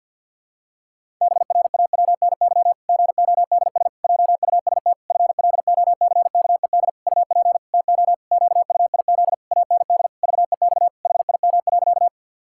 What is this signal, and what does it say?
Morse code, heard as a keyed tone, about 47 wpm, 700 Hz. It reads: BRAWNY GODS JUST FLOCKED UP TO QUIZ AND VEX HIM.
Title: Pangram Challenge v2 - 50wpm